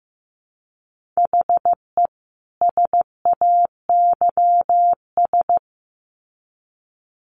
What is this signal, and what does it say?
Morse code, heard as a keyed tone, about 15 wpm, 700 Hz. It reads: HE SAYS
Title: Sets of 2 Words 15wpm